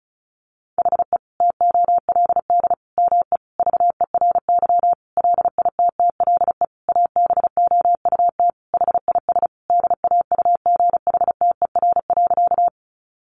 Morse code, heard as a keyed tone, about 34 wpm, 700 Hz. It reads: HE TOLD ME VERY LITTLE ABOUT HIS DAUGHTER.
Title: Sentences from Top 1500 Words 35wpm